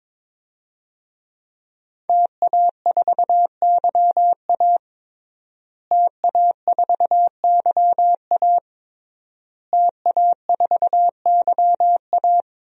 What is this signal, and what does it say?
Morse code, heard as a keyed tone, about 22 wpm, 700 Hz. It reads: TA4YA TA4YA TA4YA
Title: DXCC-100 Call Signs - 5 Characters - Repeated 3x - 22wpm